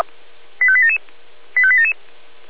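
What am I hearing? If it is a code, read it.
Five-tone selective call (ZVEI), heard as quick runs of five tones, tones of about 70 ms. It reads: 85877 85877
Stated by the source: ZVEI five-tone call 85884, received by SDR.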